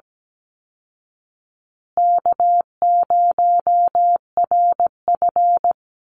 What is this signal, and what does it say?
Morse code, heard as a keyed tone, about 17 wpm, 700 Hz. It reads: K0RF
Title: Easy Call Signs Repeated 3x - 17wpm